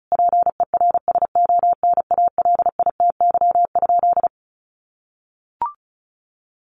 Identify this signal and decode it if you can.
Morse code, heard as a keyed tone, about 34 wpm, 700 Hz. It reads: PERSONALITY?
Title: Sentences from Top 1400 Words 35wpm